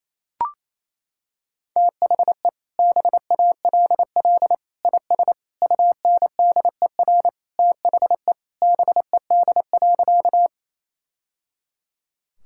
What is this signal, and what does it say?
Morse code, heard as a keyed tone, about 28 wpm, 700 Hz. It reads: THE BALL IS UNDER THE BED.
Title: Sentences from Top 1000 Words - Review Part 6 - 28wpm